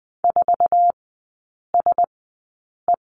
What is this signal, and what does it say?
Morse code, heard as a keyed tone, about 20 wpm, spaced as for 8 wpm, 700 Hz. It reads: 4SE
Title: 047 Lesson 9e - Number 4 - Three Characters 20wpm